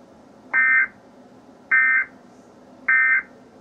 Emergency Alert System (SAME) tones, heard as harsh buzzing bursts. It is heard from 0.4 to 3.3 s.
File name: ORIGINAL-Emergency-Alert-System-Tornado-Warning-for-Knoxville-TN-March-2-2012